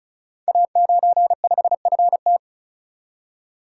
Morse code, heard as a keyed tone, about 36 wpm, 700 Hz. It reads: A95FT
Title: Code Groups - 5 Letter:Numbers - 35wpm